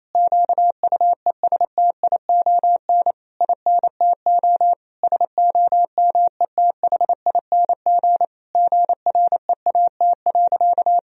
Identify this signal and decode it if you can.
Morse code, heard as a keyed tone, about 28 wpm, 700 Hz. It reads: QUESTION INTO SOMETHING GREAT.
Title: Sentences from Top 1000 Words - Review Part 2 - 28wpm